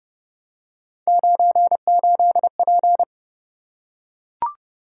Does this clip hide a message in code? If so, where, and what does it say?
Yes: Morse code from 1.1 to 3.0 s, heard as a keyed tone, about 29 wpm, 700 Hz. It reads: 98P